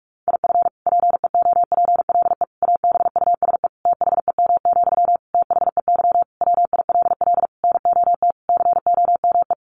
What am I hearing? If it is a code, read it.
Morse code, heard as a keyed tone, about 44 wpm, 700 Hz. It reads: IF PEOPLE ABUSE THEM, THEY WILL NOT COME
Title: Sentences from Top 1300 Words 45wpm